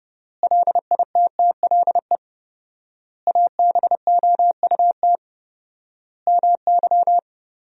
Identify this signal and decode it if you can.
Morse code, heard as a keyed tone, about 30 wpm, 700 Hz. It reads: LITTLE ABOUT MY
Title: 4x Word-Spacing - Sentences from Top 400 Words - 30wpm